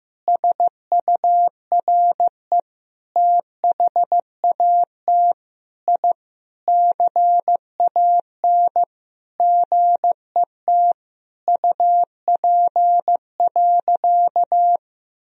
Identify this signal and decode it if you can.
Morse code, heard as a keyed tone, about 15 wpm, 700 Hz. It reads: SURE THAT I CAN GET UP.
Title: Sentences from Top 1000 Words - Review Part 3 - 15wpm